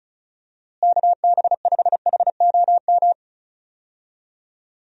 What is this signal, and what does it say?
Morse code, heard as a keyed tone, about 36 wpm, 700 Hz. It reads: KB5HOM
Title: USA Call Signs - 6 Characters - Repeated 3x - 35wpm